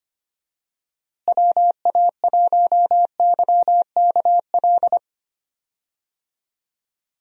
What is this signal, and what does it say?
Morse code, heard as a keyed tone, about 25 wpm, 700 Hz. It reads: WA1YKL
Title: USA Call Signs - 6 Characters - Repeated 2x - 25wpm